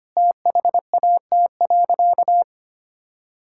Morse code, heard as a keyed tone, about 25 wpm, 700 Hz. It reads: THAT.
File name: practice / 25wpm / Sentences with 4-Word Phrases - 25wpm